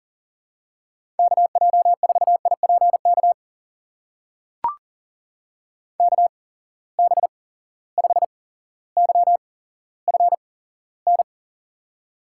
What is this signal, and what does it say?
Morse code, heard as a keyed tone, about 40 wpm, spaced as for 17 wpm, 700 Hz. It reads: KJ4IPK KB5YFN
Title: USA Call Signs - 6 Characters - 8x Character Spacing - 40wpm